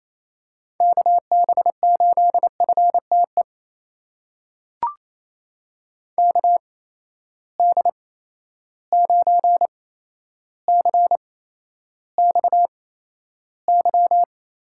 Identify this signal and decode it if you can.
Morse code, heard as a keyed tone, about 28 wpm, spaced as for 12 wpm, 700 Hz. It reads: KB8FTE KD9CXY
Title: USA Call Signs - 6 Characters - 8x Character Spacing - 28wpm